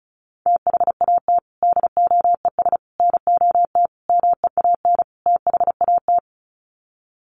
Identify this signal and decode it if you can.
Morse code, heard as a keyed tone, about 34 wpm, 700 Hz. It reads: THAT DOES NOT MEAN THAT
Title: Sets of 5 Words Encore 35wpm